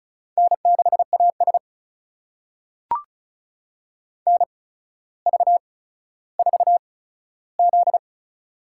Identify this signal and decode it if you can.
Morse code, heard as a keyed tone, about 36 wpm, spaced as for 15 wpm, 700 Hz. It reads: N6AS NV4Z
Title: USA Call Signs - 4 Characters - 8x Character Spacing - 35wpm